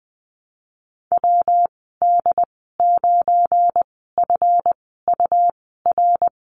Morse code, heard as a keyed tone, about 20 wpm, spaced as for 14 wpm, 700 Hz. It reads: WD9FUR
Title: 173 Lesson 28g - Number 9 - Call Signs 20wpm